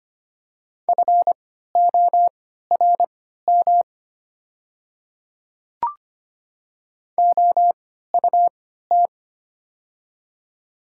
Morse code, heard as a keyed tone, about 25 wpm, spaced as for 14 wpm, 700 Hz. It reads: FORM OUT
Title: 3x Character Spacing - Top 300 Words - 25wpm